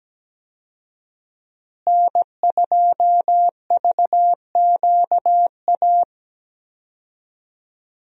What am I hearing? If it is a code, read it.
Morse code, heard as a keyed tone, about 17 wpm, 700 Hz. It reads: N2VQA
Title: USA Call Signs - 5 Characters - Repeated 2x - 17wpm